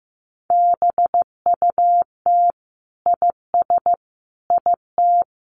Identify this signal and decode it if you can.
Morse code, heard as a keyed tone, about 15 wpm, 700 Hz. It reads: BUT IS IT
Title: Sentences from Top 1200 Words 15wpm